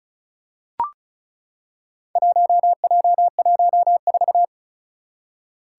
Morse code, heard as a keyed tone, about 36 wpm, 700 Hz. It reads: 1J14
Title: Four Challenging Letter-Number Combinations 35wpm